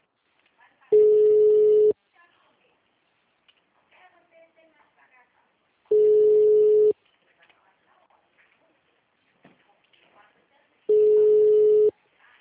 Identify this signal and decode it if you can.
Ringback tone (Europe, 425 Hz), with speech.
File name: out- -2023- - - .12708